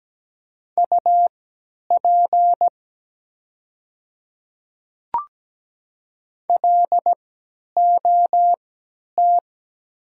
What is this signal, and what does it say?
Morse code, heard as a keyed tone, about 17 wpm, spaced as for 10 wpm, 700 Hz. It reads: UP LOT
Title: 3x Character Spacing - Top 500 Words - 17wpm